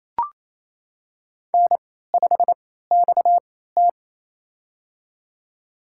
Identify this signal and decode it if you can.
Morse code, heard as a keyed tone, about 28 wpm, spaced as for 16 wpm, 700 Hz. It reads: N5XT
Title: USA Call Signs - 4 Characters - 3x Character Spacing - 28wpm